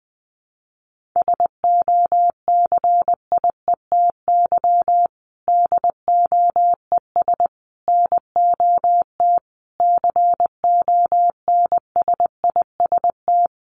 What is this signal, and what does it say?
Morse code, heard as a keyed tone, about 20 wpm, 700 Hz. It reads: SOCIETY DOES NOT CONSIST
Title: Sentences from Top 1400 Words 20wpm